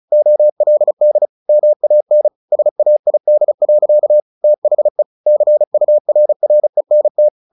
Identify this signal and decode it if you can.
Morse code, heard as a keyed tone, about 34 wpm, 600 Hz. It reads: OLD MAN SAID. THE CURRENT
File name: the old man and the sea / the old man and the sea 1 35wpm